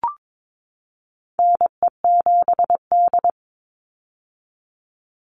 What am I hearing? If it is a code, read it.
Morse code, heard as a keyed tone, about 22 wpm, 700 Hz. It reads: NE7D
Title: Call Signs - 22wpm